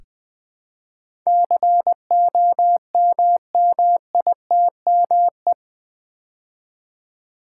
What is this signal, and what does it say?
Morse code, heard as a keyed tone, about 20 wpm, 700 Hz. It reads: COMMITME
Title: Build A Word - 4 Letters At A Time - 10-Letter Words - 20wpm